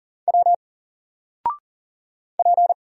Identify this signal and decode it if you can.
Morse code, heard as a keyed tone, about 39 wpm, 700 Hz. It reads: W P
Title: Single Letter-Number 40wpm